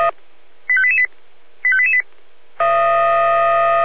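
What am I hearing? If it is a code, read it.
Five-tone selective call (ZVEI), heard as quick runs of five tones, tones of about 70 ms. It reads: 85888 85888
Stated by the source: ZVEI five-tone selective call 85359 received on SDR.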